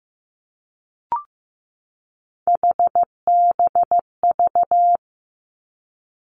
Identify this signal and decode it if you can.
Morse code, heard as a keyed tone, about 15 wpm, 700 Hz. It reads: HBV
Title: Three Challenging Letter-Number Combinations 15wpm